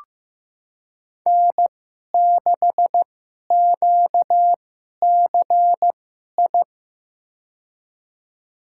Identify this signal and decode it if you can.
Morse code, heard as a keyed tone, about 15 wpm, spaced as for 11 wpm, 700 Hz. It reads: N6QCI
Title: USA Call Signs - 5 Characters - 2x Character Spacing - 15wpm